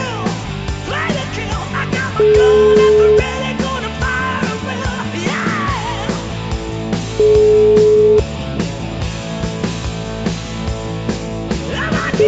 Ringback tone (Europe, 425 Hz), with noise.